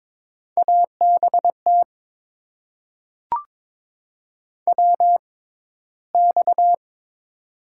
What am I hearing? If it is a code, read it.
Morse code, heard as a keyed tone, about 22 wpm, spaced as for 13 wpm, 700 Hz. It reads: ABT WX
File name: Top 90 QSO Elements - 6x Character Spacing - 22wpm